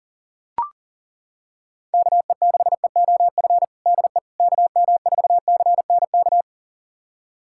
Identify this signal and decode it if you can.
Morse code, heard as a keyed tone, about 37 wpm, 700 Hz. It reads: KE6EOF DE KM4CNK
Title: USA Callsign DE Callsign - 6 Characters - 40wpm